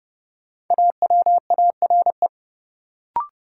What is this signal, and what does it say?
Morse code, heard as a keyed tone, about 32 wpm, 700 Hz. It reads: AWARE
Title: Sentences from Top 1200 Words - New Word Drill - 30wpm